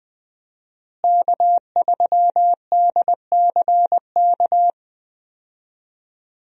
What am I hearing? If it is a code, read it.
Morse code, heard as a keyed tone, about 20 wpm, 700 Hz. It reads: K3DCK
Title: USA Call Signs - 5 Characters - Repeated 2x - 20wpm